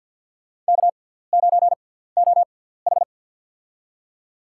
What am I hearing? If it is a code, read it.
Morse code, heard as a keyed tone, about 48 wpm, spaced as for 17 wpm, 700 Hz. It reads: K9OH
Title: USA Call Signs - 4 Characters - 6x Character Spacing - 50wpm